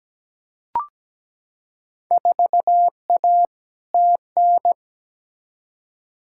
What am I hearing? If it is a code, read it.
Morse code, heard as a keyed tone, about 17 wpm, 700 Hz. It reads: 4A TN
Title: Field Day Exchanges - 17wpm